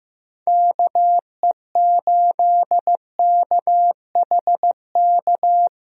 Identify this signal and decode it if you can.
Morse code, heard as a keyed tone, about 15 wpm, 700 Hz. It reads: KE8KHK
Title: USA Call Signs - 6 Characters - Repeated 3x - 15wpm